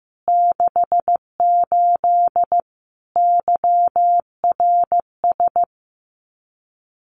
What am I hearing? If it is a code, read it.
Morse code, heard as a keyed tone, about 15 wpm, 700 Hz. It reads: 68 YRS
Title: Common QSO Phrases Encore 15wpm